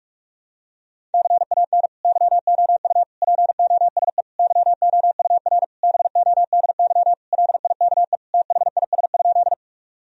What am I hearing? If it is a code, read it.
Morse code, heard as a keyed tone, about 46 wpm, 700 Hz. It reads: CAN YOU POSE YOUR BODY LIKE THIS?
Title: Sentences from Top 1000 Words - Review Part 2 - 45wpm